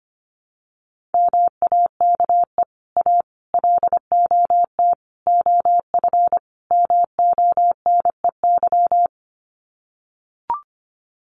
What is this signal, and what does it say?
Morse code, heard as a keyed tone, about 25 wpm, 700 Hz. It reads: MAKE A LOT OF MONEY
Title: Sets of 5 Words Encore2 25wpm